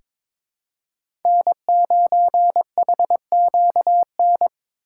Morse code, heard as a keyed tone, about 22 wpm, 700 Hz. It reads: N9HQN
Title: USA Call Signs - 5 Characters - 3x Character Spacing - 22wpm